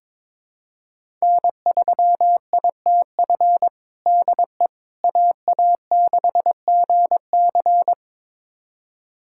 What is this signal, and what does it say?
Morse code, heard as a keyed tone, about 22 wpm, 700 Hz. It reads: N3ITF DE AA6GC
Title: USA Callsign DE Callsign - 5 Characters - 22wpm